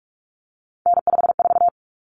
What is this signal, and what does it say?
Morse code, heard as a keyed tone, about 44 wpm, 700 Hz. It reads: N54
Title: Common Cut Numbers - 3-Digit 45wpm